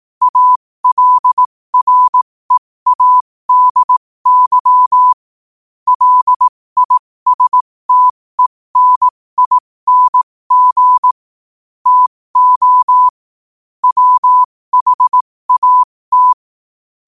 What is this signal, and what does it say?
Morse code, heard as a keyed tone, about 18 wpm, spaced as for 15 wpm, 1000 Hz. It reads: ALREADY LISTENING TO WHAT
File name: qotdfiles / 15wpm